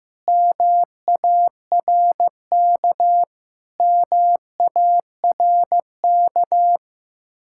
Morse code, heard as a keyed tone, about 15 wpm, 700 Hz. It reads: MARK MARK
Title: Practice QSO - Exchange 1 - Beginning - 15wpm